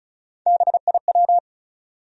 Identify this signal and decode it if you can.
Morse code, heard as a keyed tone, about 34 wpm, 700 Hz. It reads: BIW
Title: USA Call Sign Postfixes - 3 Letters 35wpm